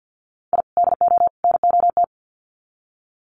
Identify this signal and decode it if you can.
Morse code, heard as a keyed tone, about 50 wpm, 700 Hz. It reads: I DO NOT
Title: Sets of 3 Words 50wpm